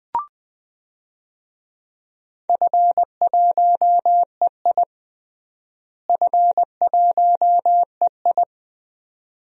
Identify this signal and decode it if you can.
Morse code, heard as a keyed tone, about 21 wpm, 700 Hz. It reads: F1EI F1EI
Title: DXCC-100 Call Signs - 4 Characters - Repeated 2x - 20wpm